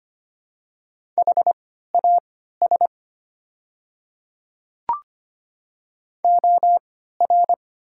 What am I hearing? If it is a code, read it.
Morse code, heard as a keyed tone, about 25 wpm, spaced as for 14 wpm, 690 Hz. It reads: HAS OR
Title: 3x Character Spacing - Top 300 Words - 25wpm